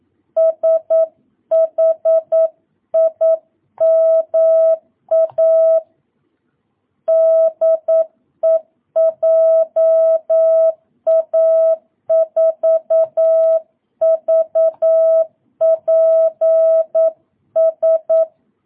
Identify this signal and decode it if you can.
Morse code, heard as a keyed tone, about 9 wpm, 650 Hz. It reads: SHIMA DE JA4VPS